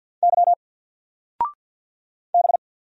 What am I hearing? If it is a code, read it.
Morse code, heard as a keyed tone, about 48 wpm, 700 Hz. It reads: Y B
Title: Single Letter-Number 50wpm